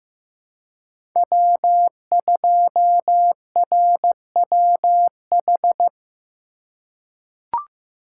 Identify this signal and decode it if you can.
Morse code, heard as a keyed tone, about 16 wpm, 700 Hz. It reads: W2RWH
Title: USA Call Signs - 5 Characters - 3x Character Spacing - 15wpm